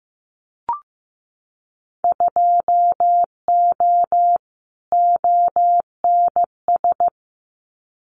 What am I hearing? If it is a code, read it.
Morse code, heard as a keyed tone, about 15 wpm, 700 Hz. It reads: 2O ONS
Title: Winter Field Day Exchanges Encore 15wpm